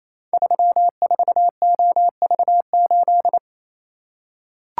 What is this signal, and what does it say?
Morse code, heard as a keyed tone, about 28 wpm, 700 Hz. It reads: 34OV8
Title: Code Groups - 5 Letter:Numbers - 28wpm